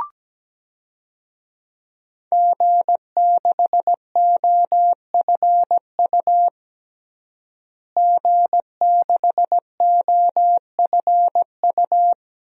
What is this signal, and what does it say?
Morse code, heard as a keyed tone, about 17 wpm, 700 Hz. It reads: G6OFU G6OFU
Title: DXCC-100 Call Signs - 5 Characters - Repeated 2x - 17wpm